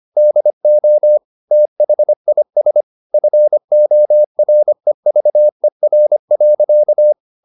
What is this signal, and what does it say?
Morse code, heard as a keyed tone, about 25 wpm, 600 Hz. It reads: DO THIS FOREVER.
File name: the old man and the sea part 1 25wpm